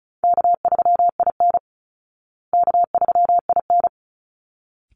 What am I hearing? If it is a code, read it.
Morse code, heard as a keyed tone, about 34 wpm, 700 Hz. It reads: K3IN K3IN
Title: Call Signs Repeated 3x - 35wpm